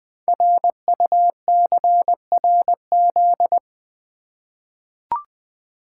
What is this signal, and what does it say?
Morse code, heard as a keyed tone, about 20 wpm, 700 Hz. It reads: RUCRZ
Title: Code Groups - 5 Letters - 20wpm